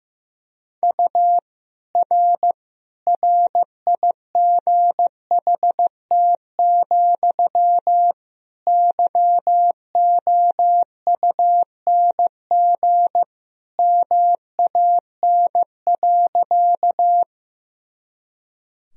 Morse code, heard as a keyed tone, about 15 wpm, 700 Hz. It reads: U R RIGHT, YOUNG MAN.
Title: Sentences from Top 400 Words and Abbreviations 15wpm